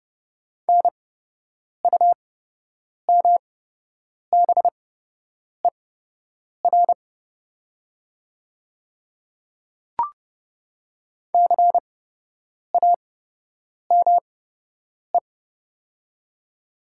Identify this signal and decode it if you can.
Morse code, heard as a keyed tone, about 30 wpm, spaced as for 8 wpm, 700 Hz. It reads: NUMBER CAME
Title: 8x Character Spacing - Top 200 Words - 30wpm